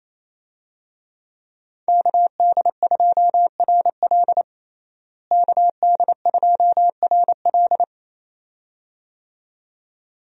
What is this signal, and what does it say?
Morse code, heard as a keyed tone, about 28 wpm, 700 Hz. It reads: KD2RL KD2RL
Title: USA Call Signs - 5 Characters - Repeated 2x - 28wpm